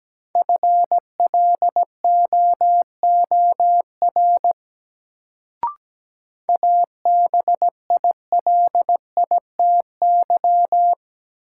Morse code, heard as a keyed tone, about 17 wpm, 700 Hz. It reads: FLOOR ABILITY
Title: Sets of 3 Words - New Word Drill - 17wpm